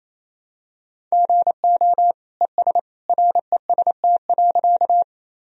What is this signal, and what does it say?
Morse code, heard as a keyed tone, about 28 wpm, 700 Hz. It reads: GO ES REST.
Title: Sentences from Top 500 Words and Abbreviations 28wpm